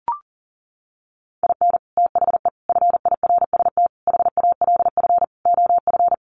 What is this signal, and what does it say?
Morse code, heard as a keyed tone, about 40 wpm, 700 Hz. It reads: IN THE FIRST HALF OF
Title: Sets of 5 Words 40wpm